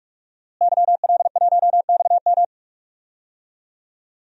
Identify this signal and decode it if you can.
Morse code, heard as a keyed tone, about 44 wpm, 700 Hz. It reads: YL1XM
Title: DXCC-100 Call Signs - 5 Characters - Repeated 2x - 45wpm